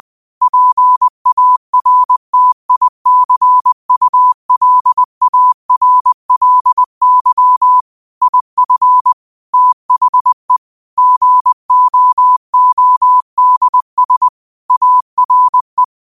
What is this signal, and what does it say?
Morse code, heard as a keyed tone, about 20 wpm, 1000 Hz. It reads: PARTICULARLY IF THE GOODS ARE
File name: qotdfiles / 20wpm